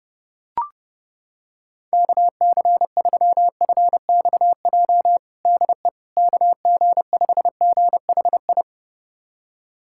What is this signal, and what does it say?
Morse code, heard as a keyed tone, about 30 wpm, 700 Hz. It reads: KC3FXJ DE KG5GHI
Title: USA Callsign DE Callsign 30wpm